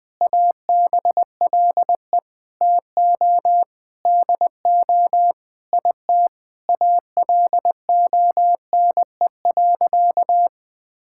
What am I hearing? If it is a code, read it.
Morse code, heard as a keyed tone, about 20 wpm, 700 Hz. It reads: ABLE TO DO IT ALONE.
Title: Sentences with 7-Word Phrases - 20wpm